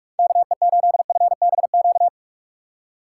Morse code, heard as a keyed tone, about 45 wpm, 700 Hz. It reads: KE8FBQ
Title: USA Call Signs - 6 Characters - 3x Character Spacing - 45wpm